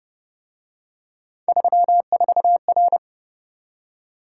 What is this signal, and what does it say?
Morse code, heard as a keyed tone, about 31 wpm, 700 Hz. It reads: 34R